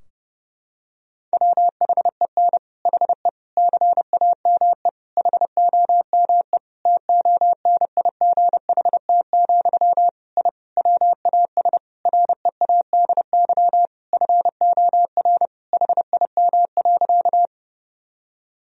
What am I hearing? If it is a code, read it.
Morse code, heard as a keyed tone, about 30 wpm, 700 Hz. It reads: WHEN HE CAME HOME TONIGHT, I WAS READY FOR HIM.
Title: Sentences from Top 1000 Words - Review Part 3 - 30wpm